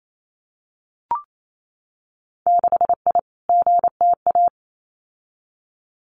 Morse code, heard as a keyed tone, about 28 wpm, 700 Hz. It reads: 6I GTA
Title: Winter Field Day Exchanges Encore 28wpm